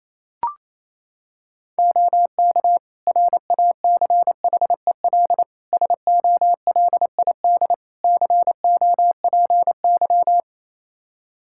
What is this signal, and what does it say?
Morse code, heard as a keyed tone, about 27 wpm, 700 Hz. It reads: OK RACHEL SOLID COPY
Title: K1USN SST QSO Practice 28wpm